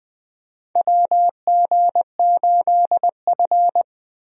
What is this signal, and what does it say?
Morse code, heard as a keyed tone, about 20 wpm, 700 Hz. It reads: WG8F
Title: USA Call Signs - 4 Characters - Repeated 2x - 20wpm